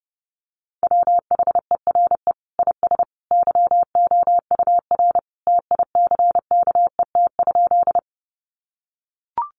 Morse code, heard as a keyed tone, about 30 wpm, 700 Hz. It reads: WHERE IS YOUR TICKET?
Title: Sentences from Top 1400 Words 30wpm